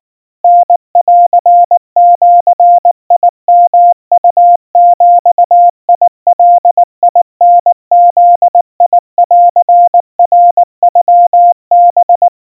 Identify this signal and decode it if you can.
Morse code (Japanese Wabun code), heard as a keyed tone, about 19 wpm, 700 Hz. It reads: タンジヨウビガタブンナノハ
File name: kawa / Morse sample / kamyu 70 cpm 700 Hz Crispy